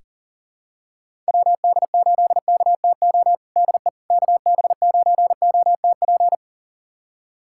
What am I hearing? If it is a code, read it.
Morse code, heard as a keyed tone, about 40 wpm, 700 Hz. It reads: WD8KTO DE KB9OTP
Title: USA Callsign DE Callsign - 6 Characters - 40wpm